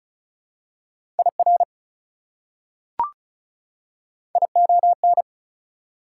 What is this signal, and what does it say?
Morse code, heard as a keyed tone, about 36 wpm, 700 Hz. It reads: IR ION
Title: Prefixes and Suffixes 35wpm